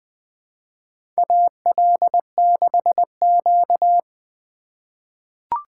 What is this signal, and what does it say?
Morse code, heard as a keyed tone, about 21 wpm, 700 Hz. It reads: AL6Q
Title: USA Call Signs - 4 Characters - 8x Character Spacing - 20wpm